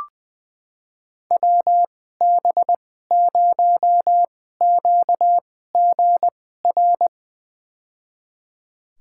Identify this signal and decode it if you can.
Morse code, heard as a keyed tone, about 20 wpm, spaced as for 15 wpm, 700 Hz. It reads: WB0QGR
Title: USA Call Signs - 6 Characters - 2x Character Spacing - 20wpm